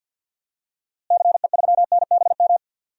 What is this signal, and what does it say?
Morse code, heard as a keyed tone, about 46 wpm, 700 Hz. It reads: CE3NBM
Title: DXCC-100 Call Signs - 6 Characters - Repeated 3x - 50wpm